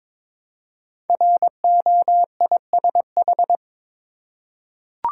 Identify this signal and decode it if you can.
Morse code, heard as a keyed tone, about 22 wpm, 700 Hz. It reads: ROISH
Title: Code Groups - 5 Letters - 22wpm